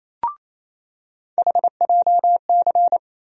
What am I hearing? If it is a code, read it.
Morse code, heard as a keyed tone, about 28 wpm, 700 Hz. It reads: HJC
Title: USA Call Sign Postfixes - 3 Letters 28wpm